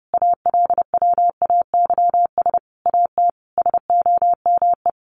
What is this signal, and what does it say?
Morse code, heard as a keyed tone, about 30 wpm, 700 Hz. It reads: ALWAYS AT SOME
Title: Sentences from Top 1200 Words 30wpm